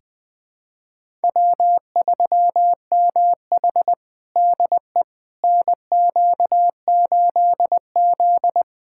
Morse code, heard as a keyed tone, about 21 wpm, 700 Hz. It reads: W3MH DE NQ8Z
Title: USA Callsign DE Callsign - 4 Characters - 20wpm